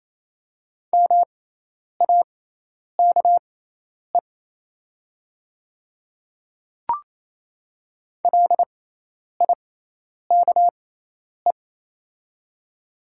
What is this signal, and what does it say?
Morse code, heard as a keyed tone, about 28 wpm, spaced as for 10 wpm, 700 Hz. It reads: MAKE LIKE
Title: 6x Character Spacing - Top 200 Words - 28wpm